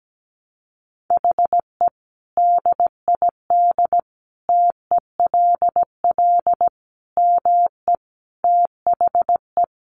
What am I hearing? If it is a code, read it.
Morse code, heard as a keyed tone, about 17 wpm, 700 Hz. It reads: HE DID TELL ME THE
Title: Sentences from Top 1000 Words 17wpm